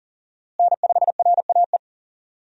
Morse code, heard as a keyed tone, about 40 wpm, 700 Hz. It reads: N5RAE
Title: USA Call Signs - 5 Characters - Repeated 3x - 40wpm